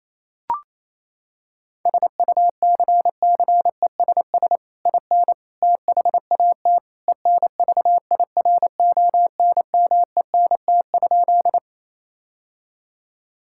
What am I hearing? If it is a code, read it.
Morse code, heard as a keyed tone, about 28 wpm, 700 Hz. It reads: SUCCESS IN THAT ENVIRONMENT?
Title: Sentences from Top 1000 Words - Review Part 5 - 28wpm